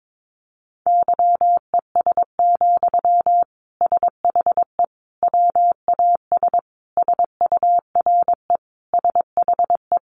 Morse code, heard as a keyed tone, about 22 wpm, 700 Hz. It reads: YES, SHE WAS SURE SHE
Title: Sentences from Top 1100 Words 22wpm